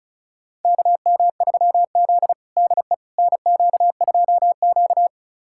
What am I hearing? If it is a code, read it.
Morse code, heard as a keyed tone, about 36 wpm, 700 Hz. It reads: KM3Z DE NQ2Q
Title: USA Callsign DE Callsign - 4 Characters - 35wpm